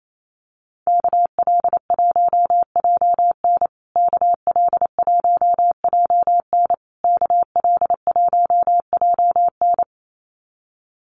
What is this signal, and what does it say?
Morse code, heard as a keyed tone, about 28 wpm, 700 Hz. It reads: KL1JN KL1JN KL1JN
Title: USA Call Signs Repeated 3x 28wpm